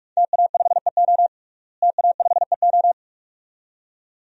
Morse code, heard as a keyed tone, about 46 wpm, 700 Hz. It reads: TA5EO TA5EO
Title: DXCC-100 Call Signs - 5 Characters - Repeated 2x - 45wpm